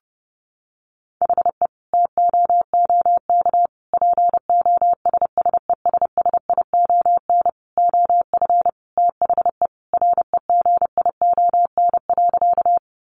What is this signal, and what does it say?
Morse code, heard as a keyed tone, about 30 wpm, 700 Hz. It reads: HE TOOK POSSESSION OF THE REGION.
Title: Sentences from Top 1500 Words 30wpm